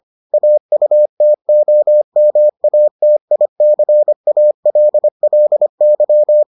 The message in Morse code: AUTOMATICALLY

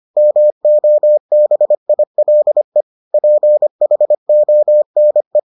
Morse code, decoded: MOBILE PHONE

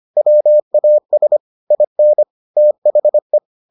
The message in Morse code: WAS IN THE

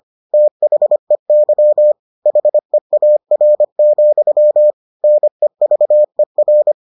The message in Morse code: THEY HEAR, NEVER

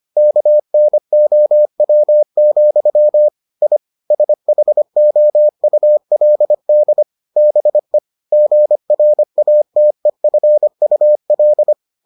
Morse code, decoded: KNOW, I SHOULD BE GRATEFUL

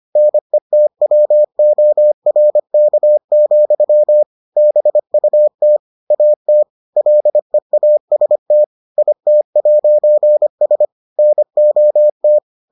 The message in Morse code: NETWORK, BUT AT LEAST IT'S NOT

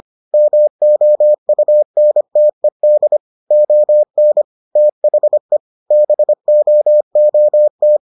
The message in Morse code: MOUNTED ON THE BOOT

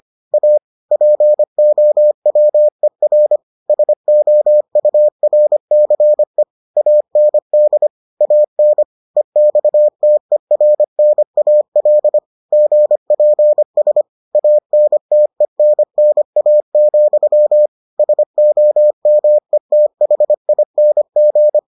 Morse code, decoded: A POWER SOURCE AND AN EXTERNAL GPS ANTENNA, SOMETHING

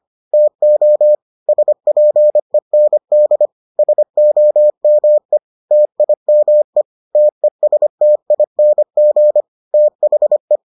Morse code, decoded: TO SPEND SOME TIME TESTING THE

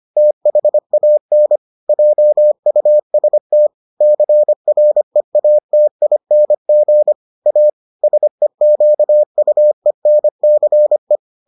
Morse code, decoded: THAN JUST CREATING A SEQUENCE

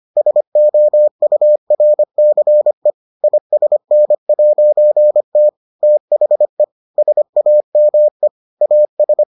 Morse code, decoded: SOURCE ISN'T THE SAME AS

